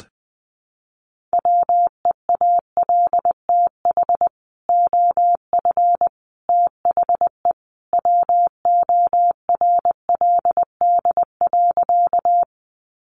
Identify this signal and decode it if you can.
Morse code, heard as a keyed tone, about 20 wpm, 700 Hz. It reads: WEALTH OF THE WORLD.
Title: Sentences from Top 1200 Words 20wpm